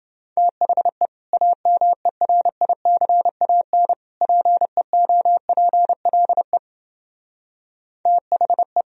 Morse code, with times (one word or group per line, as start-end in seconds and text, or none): 0.4-1.1 THE
1.3-3.9 AMERICAN
4.2-6.6 PEOPLE
8.1-8.8 THE